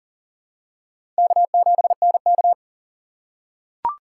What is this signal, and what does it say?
Morse code, heard as a keyed tone, about 40 wpm, 700 Hz. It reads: K7NK
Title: USA Call Signs - 4 Characters - 4x Character Spacing - 40wpm